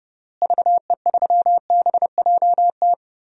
Morse code, heard as a keyed tone, about 30 wpm, 700 Hz. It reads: VE3BJT
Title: DXCC-100 Call Signs - 6 Characters - Repeated 3x - 30wpm